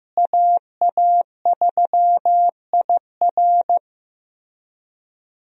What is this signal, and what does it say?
Morse code, heard as a keyed tone, about 15 wpm, 700 Hz. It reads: AA3IR
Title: USA Call Signs - 5 Characters - Repeated 3x - 15wpm